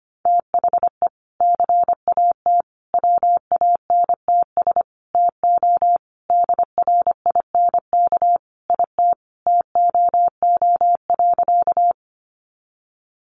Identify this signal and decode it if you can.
Morse code, heard as a keyed tone, about 25 wpm, 700 Hz. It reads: THE CAT WANTS TO DRINK IT TOO.
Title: Additional Easy Phrases 25wpm